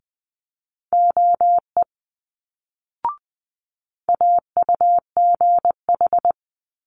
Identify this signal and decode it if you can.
Morse code, heard as a keyed tone, about 19 wpm, 700 Hz. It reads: OE AUGH
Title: Multi-letter Phonemes Quick Fire 20wpm